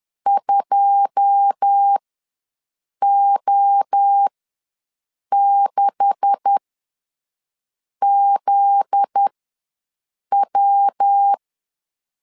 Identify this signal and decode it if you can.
Morse code, heard as a keyed tone, about 10 wpm, spaced as for 6 wpm, 800 Hz. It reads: 2O6ZW
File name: LEZIONI / 10 SIGNS / 10A SLASH ws 10wpm 4121